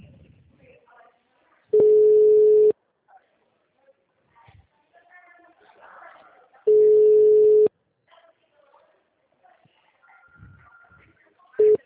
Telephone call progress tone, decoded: ringback tone

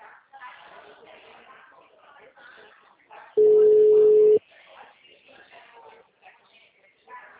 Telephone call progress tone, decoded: ringback tone

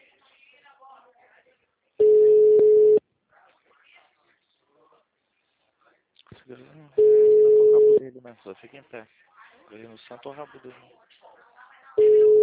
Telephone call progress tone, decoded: ringback tone

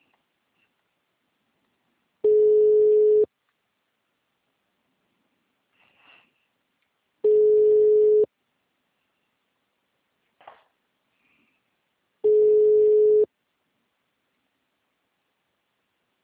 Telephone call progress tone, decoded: ringback tone

